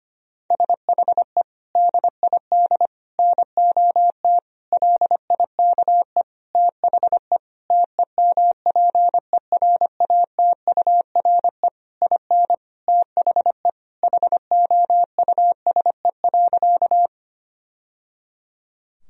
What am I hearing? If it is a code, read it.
Morse code, heard as a keyed tone, about 25 wpm, 700 Hz. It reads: SHE DID NOT LIKE THE TEMPERATURE IN THE HOUSE.
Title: Sentences from Top 1000 Words - Review Part 4 - 25wpm